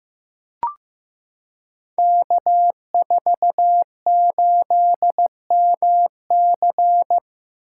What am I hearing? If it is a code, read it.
Morse code, heard as a keyed tone, about 15 wpm, 700 Hz. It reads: K48MC
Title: Code Groups - 5 Letter:Numbers - 15wpm